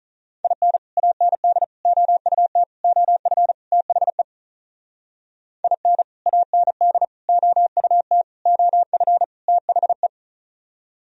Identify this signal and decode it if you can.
Morse code, heard as a keyed tone, about 45 wpm, 700 Hz. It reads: IN AND OUT OF THE IN AND OUT OF THE